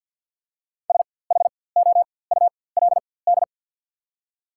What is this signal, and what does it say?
Morse code, heard as a keyed tone, about 48 wpm, spaced as for 23 wpm, 690 Hz. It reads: SHOULD
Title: 4x Character Spacing - Top 300 Words - 50wpm